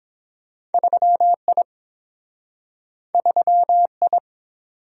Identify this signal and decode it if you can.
Morse code, heard as a keyed tone, about 30 wpm, 700 Hz. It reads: 3I 3I